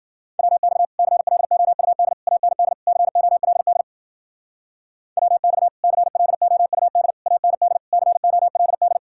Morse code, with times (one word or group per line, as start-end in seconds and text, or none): 0.4-0.9 WX
1.0-2.1 CLOUD
2.3-2.7 AND
2.9-3.8 COLD
5.2-5.7 WX
5.8-7.1 CLOUD
7.3-7.8 AND
7.9-9.0 COLD